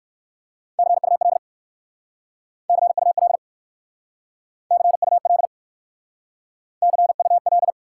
Morse code, heard as a keyed tone, about 58 wpm, 700 Hz. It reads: CUL CUL CUL CUL